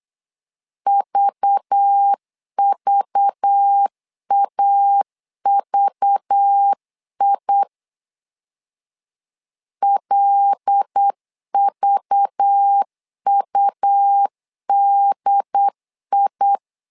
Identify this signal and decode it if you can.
Morse code, heard as a keyed tone, about 8 wpm, 800 Hz. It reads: VVAVI LVUDI
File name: LEZIONI / 5 LBVC / 5 LBV 8wpm 2837